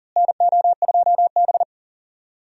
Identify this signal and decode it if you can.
Morse code, heard as a keyed tone, about 40 wpm, 700 Hz. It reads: NO2B
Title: USA Call Signs - 4 Characters - Repeated 3x - 40wpm